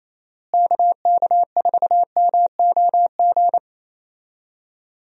Morse code, heard as a keyed tone, about 28 wpm, 700 Hz. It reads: KK4MOG
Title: USA Call Signs - 6 Characters - Repeated 3x - 28wpm